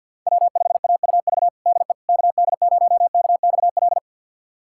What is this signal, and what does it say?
Morse code, heard as a keyed tone, about 50 wpm, 700 Hz. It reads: W5AUV DE KD0KXL
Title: USA Callsign DE Callsign 50wpm